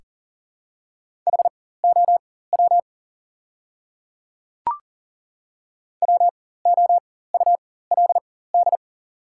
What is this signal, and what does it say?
Morse code, heard as a keyed tone, about 40 wpm, spaced as for 19 wpm, 700 Hz. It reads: HOW WOULD
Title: 4x Character Spacing - Top 200 Words - 40wpm